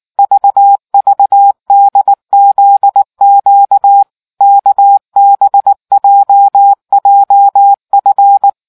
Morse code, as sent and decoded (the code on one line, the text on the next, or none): ...- ...- -.. --.. --.- / -.- -... .--- .--- ..-.
VVDZQ KBJJF